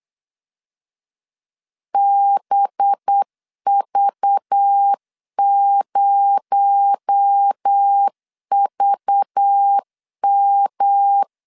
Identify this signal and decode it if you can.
Morse code, heard as a keyed tone, about 8 wpm, 800 Hz. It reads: BV0VM
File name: LEZIONI / 6 PKX0 / 6 PKX0 8wpm 3341